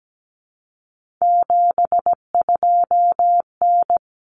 Morse code, heard as a keyed tone, about 17 wpm, 700 Hz. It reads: 72N
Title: Common Cut Numbers - 3-Digit 17wpm